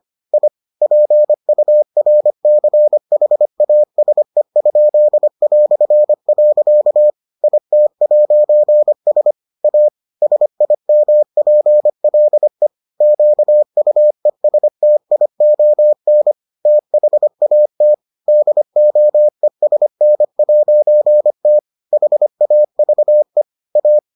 Morse code, heard as a keyed tone, about 25 wpm, 600 Hz. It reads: I PURCHASE?". IT'S A SIMPLE QUESTION THAT DOESN'T HAVE A